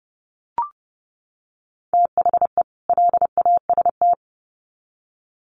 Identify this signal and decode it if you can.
Morse code, heard as a keyed tone, about 30 wpm, 700 Hz. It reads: THE LAST
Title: Sets of 2 Words 30wpm